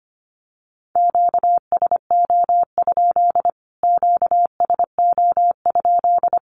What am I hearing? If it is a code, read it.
Morse code, heard as a keyed tone, about 25 wpm, 700 Hz. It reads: QSO? QSO?
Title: Q-Codes Repeated 3x 25wpm